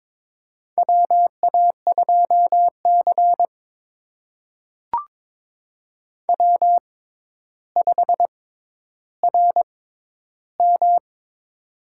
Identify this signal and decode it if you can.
Morse code, heard as a keyed tone, about 22 wpm, spaced as for 11 wpm, 700 Hz. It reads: WA2C W5RM
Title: USA Call Signs - 4 Characters - 6x Character Spacing - 22wpm